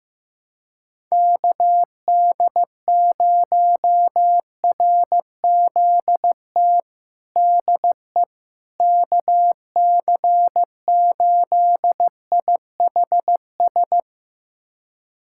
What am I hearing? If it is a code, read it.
Morse code, heard as a keyed tone, about 15 wpm, 700 Hz. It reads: KD0RZT DE KC8IHS
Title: USA Callsign DE Callsign 15wpm